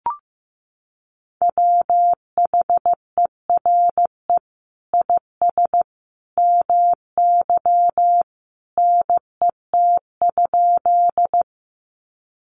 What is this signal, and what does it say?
Morse code, heard as a keyed tone, about 15 wpm, 700 Hz. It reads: WHERE IS MY NET?
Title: Sentences from Top 1400 Words 15wpm